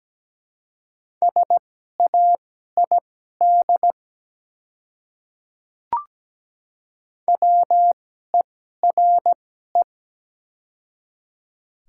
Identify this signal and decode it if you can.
Morse code, heard as a keyed tone, about 17 wpm, spaced as for 12 wpm, 690 Hz. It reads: SAID WERE
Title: 2x Character Spacing - Top 100 Words - 17wpm